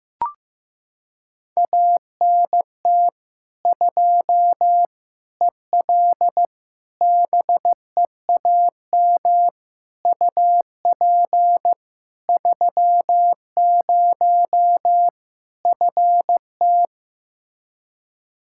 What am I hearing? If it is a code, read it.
Morse code, heard as a keyed tone, about 16 wpm, 700 Hz. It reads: ANT 2 EL BEAM UP 30 FT
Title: Practice QSO - Exchange 2 - Info - 15wpm